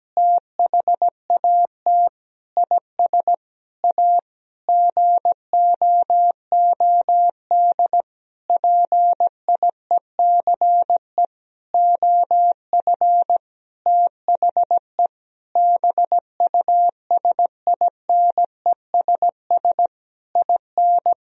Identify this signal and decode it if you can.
Morse code, heard as a keyed tone, about 17 wpm, 700 Hz. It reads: THAT IS A GOOD PIECE OF THE BUSINESS IN
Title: Sentences from Top 1000 Words - Review Part 2 - 17wpm